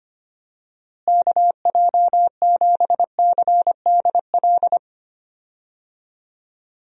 Morse code, heard as a keyed tone, about 25 wpm, 700 Hz. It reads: KJ7CDL
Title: USA Call Signs - 6 Characters - Repeated 3x - 25wpm